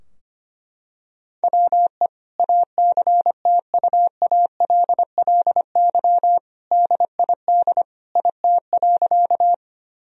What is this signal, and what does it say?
Morse code, heard as a keyed tone, about 25 wpm, 700 Hz. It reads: WE ACTUALLY DID IT.
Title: Sentences from Top 1000 Words - Review Part 4 - 25wpm